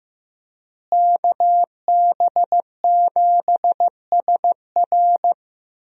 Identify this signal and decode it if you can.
Morse code, heard as a keyed tone, about 15 wpm, 700 Hz. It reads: KB7SR
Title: USA Call Signs - 5 Characters - Repeated 3x - 15wpm